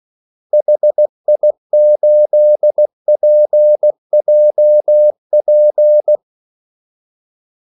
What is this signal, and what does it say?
Morse code, heard as a keyed tone, about 16 wpm, 600 Hz. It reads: HI8PJP
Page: Kaldesignaler 16 wpm